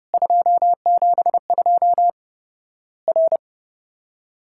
Morse code, heard as a keyed tone, about 32 wpm, 700 Hz. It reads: 272 R
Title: Worked All Europe DX Contest - QTC Exchange - 30wpm